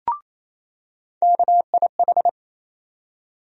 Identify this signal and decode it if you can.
Morse code, heard as a keyed tone, about 28 wpm, 700 Hz. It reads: KIH
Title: USA Call Sign Postfixes - 3 Letters 28wpm